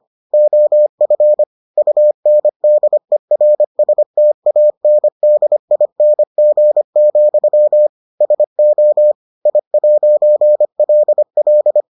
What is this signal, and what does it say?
Morse code, heard as a keyed tone, about 25 wpm, 600 Hz. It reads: OF UNDERSTANDING, SO I'LL